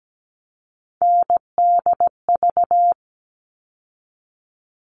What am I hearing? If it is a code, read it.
Morse code, heard as a keyed tone, about 17 wpm, 700 Hz. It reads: NDV
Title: All Cut Numbers - 3-Digit 17wpm